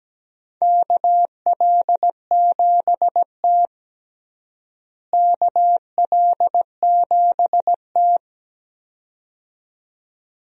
Morse code, heard as a keyed tone, about 17 wpm, 700 Hz. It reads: KL7T KL7T
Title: DXCC-100 Call Signs - 4 Characters - Repeated 2x - 17wpm